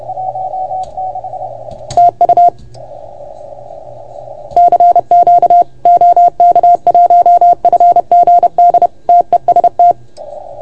Morse code, heard as a keyed tone, about 30 wpm, 690 Hz. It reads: TU CQ OK1FGD TEST
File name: Archiv / CUC381 / OK1FGD CUC 381